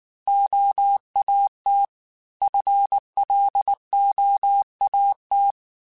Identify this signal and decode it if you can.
Morse code, heard as a keyed tone, about 19 wpm, 800 Hz. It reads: OAT FLOAT
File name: FL / FL mp3s / Fast FL @18